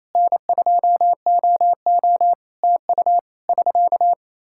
Morse code, heard as a keyed tone, about 28 wpm, 700 Hz. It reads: N2OO TU <SK>
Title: Practice QSO - Exchange 4 - Ending - 28wpm